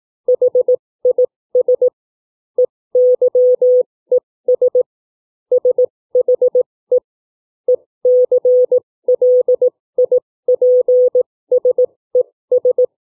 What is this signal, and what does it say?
Morse code, heard as a keyed tone, about 18 wpm, spaced as for 15 wpm, 500 Hz. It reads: HIS EYES SHE ECLIPSES
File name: CW / book reading 2